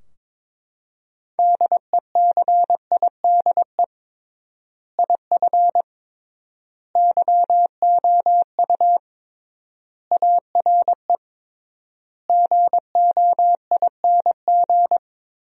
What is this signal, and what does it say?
Morse code, heard as a keyed tone, about 22 wpm, 700 Hz. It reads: DECIDE IF YOU ARE GOING
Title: 3x Word-Spacing - Sentences from Top 500 Words - 22wpm